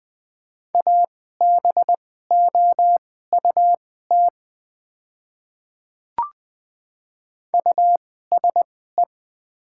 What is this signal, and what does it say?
Morse code, heard as a keyed tone, about 20 wpm, spaced as for 14 wpm, 700 Hz. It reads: ABOUT USE
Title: 2x Character Spacing - Top 100 Words - 20wpm